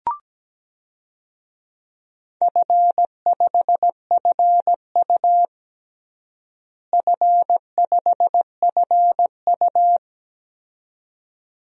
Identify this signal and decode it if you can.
Morse code, heard as a keyed tone, about 17 wpm, 700 Hz. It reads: F5FU F5FU
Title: DXCC-100 Call Signs - 4 Characters - Repeated 2x - 17wpm